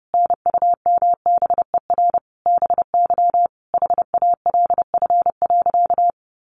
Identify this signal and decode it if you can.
Morse code, heard as a keyed tone, about 30 wpm, 700 Hz. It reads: NUMBER BY HALF.
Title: Sentences from Top 1100 Words 30wpm